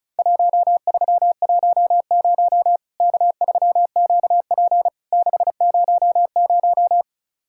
Morse code, heard as a keyed tone, about 36 wpm, 700 Hz. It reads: 1310 K3QP 600
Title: Worked All Europe DX Contest - QTC Exchange - 35wpm